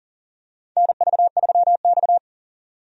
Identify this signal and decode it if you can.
Morse code, heard as a keyed tone, about 40 wpm, 700 Hz. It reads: NV3X
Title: USA Call Signs - 4 Characters - 40wpm